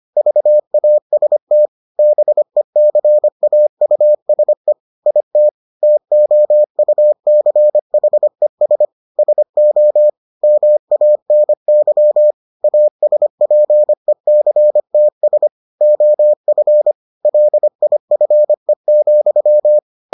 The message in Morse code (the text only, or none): VAST BECAUSE IT TOUCHES SO MANY ASPECTS OF LIFE,